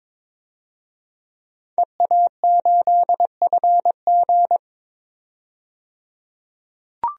Morse code, heard as a keyed tone, about 22 wpm, 700 Hz. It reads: EA8FG
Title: DXCC-100 Call Signs - 5 Characters - 22wpm